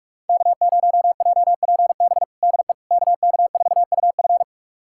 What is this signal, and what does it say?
Morse code, heard as a keyed tone, about 44 wpm, 700 Hz. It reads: K0JPB DE KK4UF
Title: USA Callsign DE Callsign - 5 Characters - 45wpm